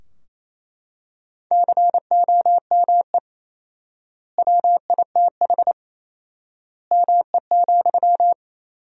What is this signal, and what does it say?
Morse code, heard as a keyed tone, about 28 wpm, 700 Hz. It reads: COME WITH ME,
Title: 4x Word-Spacing - Sentences from Top 200 Words - 28wpm